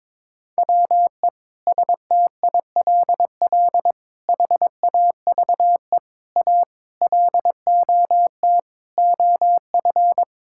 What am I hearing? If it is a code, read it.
Morse code, heard as a keyed tone, about 22 wpm, 700 Hz. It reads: WE STILL HAVE A LOT OF
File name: practice / 22wpm / Sentences with 8-Word Phrases - 22wpm